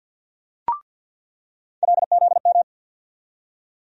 Morse code, heard as a keyed tone, about 47 wpm, 700 Hz. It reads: LZM
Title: USA Call Sign Postfixes - 3 Letters 50wpm